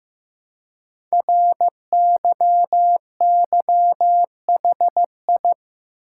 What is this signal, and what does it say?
Morse code, heard as a keyed tone, about 15 wpm, 700 Hz. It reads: RYYHI
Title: Code Groups - 5 Letters - 15wpm